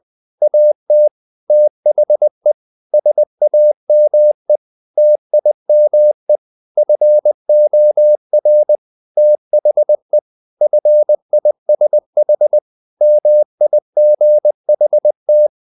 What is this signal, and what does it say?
Morse code, heard as a keyed tone, about 20 wpm, 600 Hz. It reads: AT THE SAME TIME FOR THE FISH MIGHT
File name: the old man and the sea / the old man and the sea part 1 20wpm